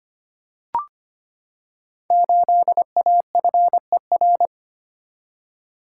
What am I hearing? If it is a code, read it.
Morse code, heard as a keyed tone, about 25 wpm, 700 Hz. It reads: 8AFER
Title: Code Groups - 5 Letter:Numbers - 25wpm